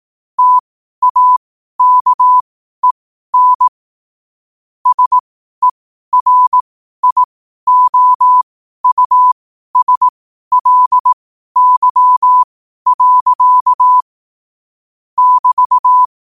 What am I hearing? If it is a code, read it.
Morse code, heard as a keyed tone, about 18 wpm, spaced as for 12 wpm, 1000 Hz. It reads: TAKEN SERIOUSLY. <BT>
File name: qotdfiles / 12wpm